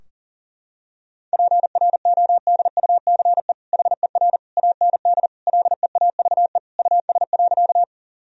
Morse code, heard as a keyed tone, about 40 wpm, 700 Hz. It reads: PRODUCE HER AND LEAVE US.
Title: Sentences from Top 1000 Words - Review Part 3 - 40wpm